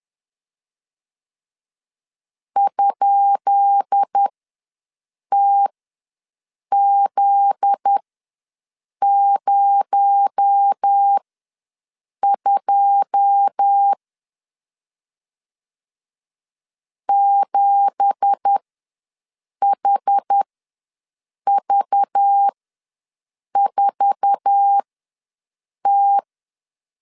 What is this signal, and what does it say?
Morse code, heard as a keyed tone, about 10 wpm, spaced as for 6 wpm, 800 Hz. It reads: ?TZ02 7HV4T
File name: LEZIONI / 10 SIGNS / 10B INT ws 10wpm 4134